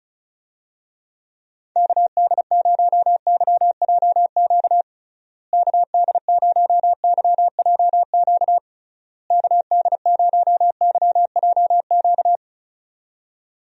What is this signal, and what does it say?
Morse code, heard as a keyed tone, about 36 wpm, 700 Hz. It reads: KD0YJQ KD0YJQ KD0YJQ
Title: USA Call Signs - 6 Characters - Repeated 3x - 35wpm